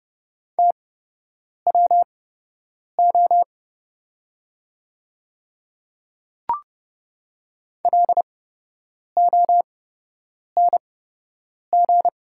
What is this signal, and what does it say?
Morse code, heard as a keyed tone, about 30 wpm, spaced as for 8 wpm, 700 Hz. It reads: TWO LONG
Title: 8x Character Spacing - Top 100 Words - 30wpm